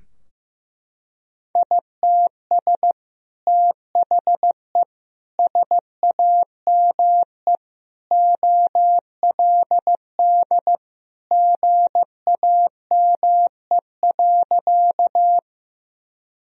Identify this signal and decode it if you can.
Morse code, heard as a keyed tone, about 16 wpm, 700 Hz. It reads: ITS THE SAME OLD GAME.
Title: Sentences from Top 1000 Words - Review Part 3 - 15wpm